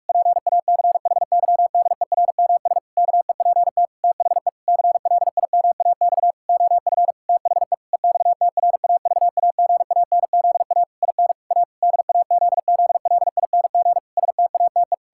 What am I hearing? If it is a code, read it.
Morse code, heard as a keyed tone, about 46 wpm, 700 Hz. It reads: PACHYDERMS KEPT THE CLIMAX OF THE EXTRAVAGANZA IN A DAZZLING STATE
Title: Pangram Challenge v2 - 45wpm